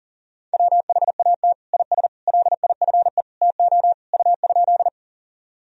Morse code, heard as a keyed tone, about 40 wpm, 700 Hz. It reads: WHAT IS LIFE TO U?
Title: Sentences from Top 300 Words and Abbreviations 40wpm